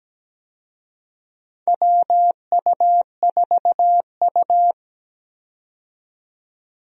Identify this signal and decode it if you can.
Morse code, heard as a keyed tone, about 17 wpm, 700 Hz. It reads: WU4U
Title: USA Call Signs - 4 Characters - 17wpm